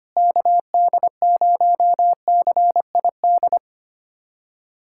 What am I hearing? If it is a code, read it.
Morse code, heard as a keyed tone, about 25 wpm, 700 Hz. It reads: KD0CID
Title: USA Callsign DE Callsign 25wpm